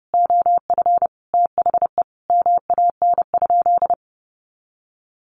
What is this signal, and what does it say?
Morse code, heard as a keyed tone, about 30 wpm, 700 Hz. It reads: OF THE MAN?
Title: Sentences from Top 1300 Words 30wpm